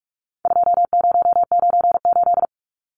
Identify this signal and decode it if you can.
Morse code, heard as a keyed tone, about 44 wpm, 700 Hz. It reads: 2098
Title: Numbers - 4-Digit 45wpm